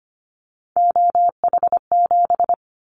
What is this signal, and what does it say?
Morse code, heard as a keyed tone, about 25 wpm, 700 Hz. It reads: OH7
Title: Callsign Prefixes for Top 50 DXCC - 25wpm